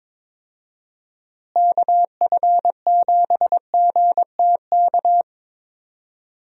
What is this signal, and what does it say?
Morse code, heard as a keyed tone, about 22 wpm, 700 Hz. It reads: KF7GTK
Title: USA Call Signs - 6 Characters - Repeated 2x - 22wpm